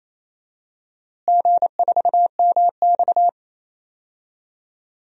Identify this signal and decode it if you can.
Morse code, heard as a keyed tone, about 28 wpm, 700 Hz. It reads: G4MX
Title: DXCC-100 Call Signs - 4 Characters - Repeated 3x - 28wpm